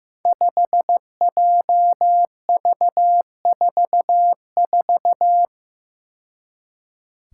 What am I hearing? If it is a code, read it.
Morse code, heard as a keyed tone, about 15 wpm, 700 Hz. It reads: 5JV44
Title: Five Challenging Letter-Number Combinations 15wpm